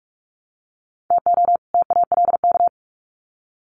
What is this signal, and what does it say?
Morse code, heard as a keyed tone, about 44 wpm, 700 Hz. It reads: TO TALK
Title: Sets of 2 Words 45wpm